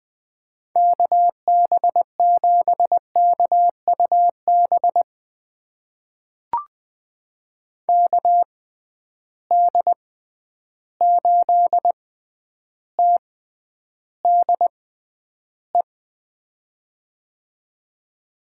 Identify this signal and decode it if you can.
Morse code, heard as a keyed tone, about 20 wpm, spaced as for 10 wpm, 700 Hz. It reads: KB7KUB KD8TDE
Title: USA Call Signs - 6 Characters - 6x Character Spacing - 20wpm